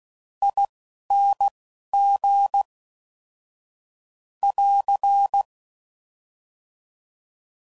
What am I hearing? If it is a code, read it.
Morse code, heard as a keyed tone, about 16 wpm, spaced as for 11 wpm, 800 Hz. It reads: ING <AR>
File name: Suffixes / Special INGs / Fast INGs